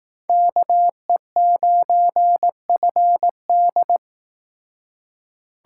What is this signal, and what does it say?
Morse code, heard as a keyed tone, about 18 wpm, 700 Hz. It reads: KE9FD
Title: Speed Racing - USA 2x2 Call Signs 15wpm